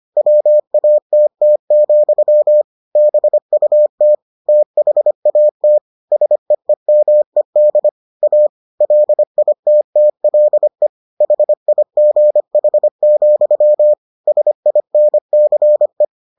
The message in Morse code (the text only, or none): WATT, BUT THAT SEEMED A LITTLE HIGH, SINCE